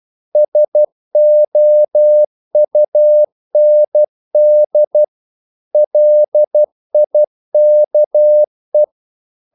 Morse code, heard as a keyed tone, about 12 wpm, 600 Hz. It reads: SOUND LIKE
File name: mp3 files / the gardener - 12wpm